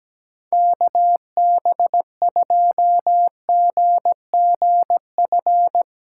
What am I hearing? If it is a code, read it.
Morse code, heard as a keyed tone, about 18 wpm, 700 Hz. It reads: KB2GGF
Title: Speed Racing - USA 2x3 Call Signs 17wpm